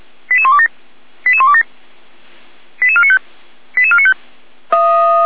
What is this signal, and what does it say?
Five-tone selective call (ZVEI), heard as quick runs of five tones, tones of about 70 ms. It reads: 88127 88127 88495 88495